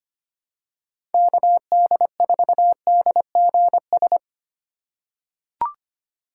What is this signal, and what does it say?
Morse code, heard as a keyed tone, about 25 wpm, 690 Hz. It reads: KD4DGS
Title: USA Call Signs - 6 Characters - 2x Character Spacing - 25wpm